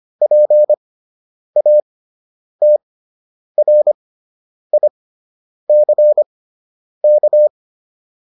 Morse code, heard as a keyed tone, about 25 wpm, spaced as for 9 wpm, 600 Hz. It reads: PATRICK
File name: QSO 105 09